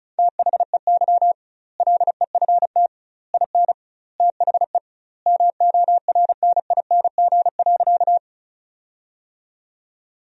Morse code, heard as a keyed tone, about 34 wpm, 700 Hz. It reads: THEY LEFT IN THE MORNING.
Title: 2x Word-Spacing - Sentences from Top 400 Words - 35wpm